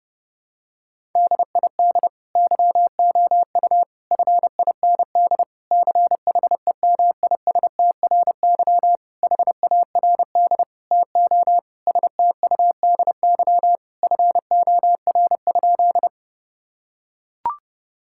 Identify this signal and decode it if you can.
Morse code, heard as a keyed tone, about 30 wpm, 700 Hz. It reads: DID YOU FIND CHEMISTRY HARD TO STUDY FOR?
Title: Sentences from Top 1000 Words - Review Part 6 - 30wpm